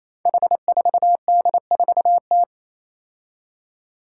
Morse code, heard as a keyed tone, about 27 wpm, 700 Hz. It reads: H4D4T
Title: Five Challenging Letter-Number Combinations 28wpm